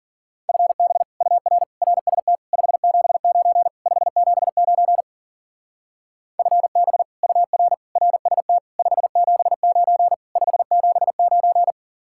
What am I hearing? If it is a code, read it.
Morse code, heard as a keyed tone, about 50 wpm, 700 Hz. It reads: FB UR RST 579 579 FB UR RST 579 579